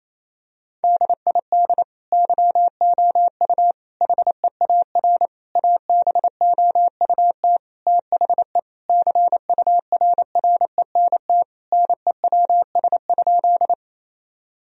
Morse code, heard as a keyed tone, about 28 wpm, 700 Hz. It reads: DID YOU HEAR ABOUT THE CURRENT NEWS?
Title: Sentences from Top 1000 Words - Review Part 4 - 28wpm